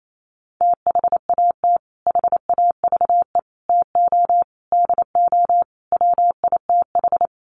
Morse code, heard as a keyed tone, about 28 wpm, 700 Hz. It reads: THAT HAVE TO DO WITH
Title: Sets of 5 Words Encore2 28wpm